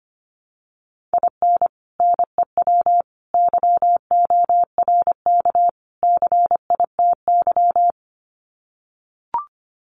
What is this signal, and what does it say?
Morse code, heard as a keyed tone, about 25 wpm, 700 Hz. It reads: IN NEW YORK CITY
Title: Sets of 4 Words 25wpm